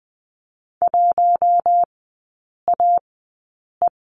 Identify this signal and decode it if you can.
Morse code, heard as a keyed tone, about 20 wpm, spaced as for 8 wpm, 700 Hz. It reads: 1AE
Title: 067 Lesson 12e - Letter d - Three Characters 20wpm